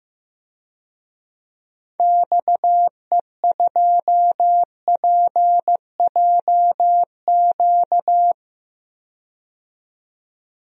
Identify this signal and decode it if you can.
Morse code, heard as a keyed tone, about 15 wpm, 700 Hz. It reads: XE2PJQ
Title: DXCC-100 Call Signs - 6 Characters - 15wpm